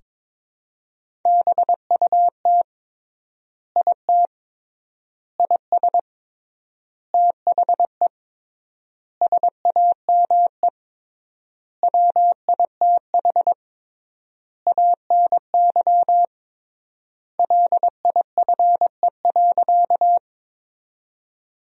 Morse code, heard as a keyed tone, about 22 wpm, 700 Hz. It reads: BUT IT IS THE SAME WITH ANY LIFE.
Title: 3x Word-Spacing - Sentences from Top 300 Words - 22wpm